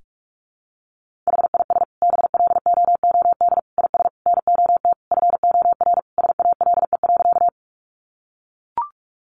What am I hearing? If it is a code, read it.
Morse code, heard as a keyed tone, about 44 wpm, 700 Hz. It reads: HIS BLOOD IS NOT FOR SALE.
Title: Sentences from Top 1400 Words 45wpm